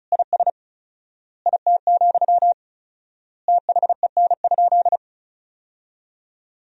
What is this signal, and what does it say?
Morse code, heard as a keyed tone, about 34 wpm, 700 Hz. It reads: IS IT, THEN?
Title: 4x Word-Spacing - Sentences from Top 200 Words - 35wpm